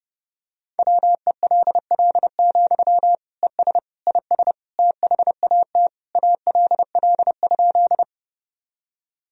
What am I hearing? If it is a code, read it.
Morse code, heard as a keyed tone, about 30 wpm, 700 Hz. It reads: WELL, ES IS THAT ALL?
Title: Sentences from Top 200 Words and Abbreviations 30wpm